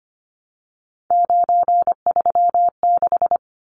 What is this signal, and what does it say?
Morse code, heard as a keyed tone, about 25 wpm, 700 Hz. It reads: 936
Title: Numbers - 3-Digit 25wpm